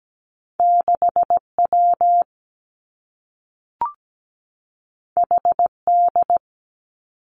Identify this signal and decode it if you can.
Morse code, heard as a keyed tone, about 17 wpm, 700 Hz. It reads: 6W HD
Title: Two Challenging Letter-Number Combinations 17wpm